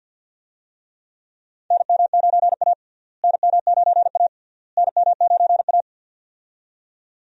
Morse code, heard as a keyed tone, about 50 wpm, 700 Hz. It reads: NM9A NM9A NM9A
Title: USA Call Signs - 4 Characters - Repeated 3x - 50wpm